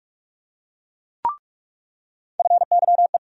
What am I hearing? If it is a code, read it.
Morse code, heard as a keyed tone, about 47 wpm, 700 Hz. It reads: FYE
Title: USA Call Sign Postfixes - 3 Letters 45wpm